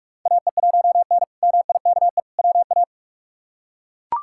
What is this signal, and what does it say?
Morse code, heard as a keyed tone, about 42 wpm, 700 Hz. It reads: AE1N MIKE WA
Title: K1USN SST QSO Practice - 45wpm